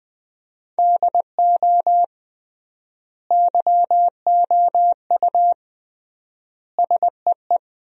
Morse code, heard as a keyed tone, about 21 wpm, 700 Hz. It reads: DO YOU SEE
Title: 3x Word-Spacing - Sentences from Top 100 Words - 20wpm